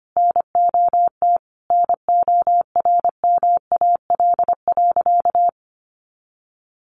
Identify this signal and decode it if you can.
Morse code, heard as a keyed tone, about 25 wpm, 700 Hz. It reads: NOT NORMAL.
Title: Sentences from Top 1200 Words 25wpm